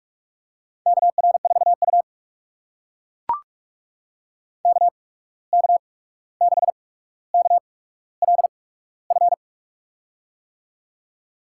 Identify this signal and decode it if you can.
Morse code, heard as a keyed tone, about 44 wpm, spaced as for 17 wpm, 700 Hz. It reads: KR4U KK6KLF
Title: USA Call Signs - 8x Character Spacing - 45wpm